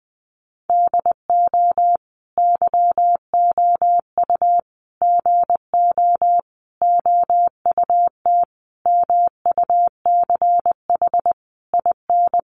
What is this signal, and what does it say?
Morse code, heard as a keyed tone, about 20 wpm, 700 Hz. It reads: DO YOU GO OUT MUCH IN
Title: Sentences from Top 700 Words 20wpm